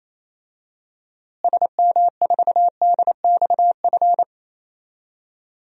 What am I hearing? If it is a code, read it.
Morse code, heard as a keyed tone, about 27 wpm, 700 Hz. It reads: SM4DXF
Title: DXCC-100 Call Signs - 6 Characters - Repeated 2x - 28wpm